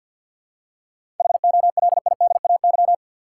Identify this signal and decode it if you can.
Morse code, heard as a keyed tone, about 52 wpm, 700 Hz. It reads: HOLIDAY
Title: Seven Letter Words - 50wpm